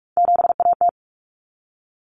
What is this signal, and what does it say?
Morse code, heard as a keyed tone, about 44 wpm, 700 Hz. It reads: 7AT
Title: Common Cut Numbers - 3-Digit 45wpm